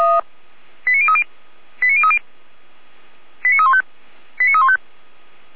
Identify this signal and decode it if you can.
Five-tone selective call (ZVEI), heard as quick runs of five tones, tones of about 70 ms. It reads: 89030 89030 89315 89315
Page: /455/SDR/ZVEI/